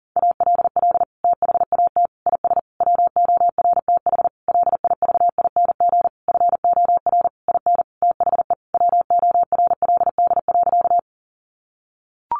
Morse code, heard as a keyed tone, about 40 wpm, 700 Hz. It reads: ALL THAT IS WORTH LIVING FOR IN THE WORLD.
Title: Sentences from Top 1100 Words 40wpm